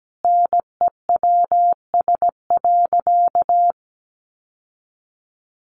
Morse code, heard as a keyed tone, about 17 wpm, 700 Hz. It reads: NEWS.
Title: Sentences from Top 800 Words 17wpm